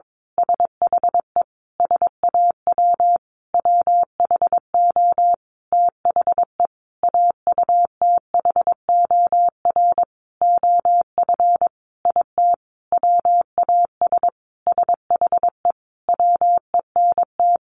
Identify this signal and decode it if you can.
Morse code, heard as a keyed tone, about 22 wpm, 700 Hz. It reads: SHE SAW WHO THE AUTHOR OF IT WAS SHE WENT
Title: Sentences from Top 1200 Words 22wpm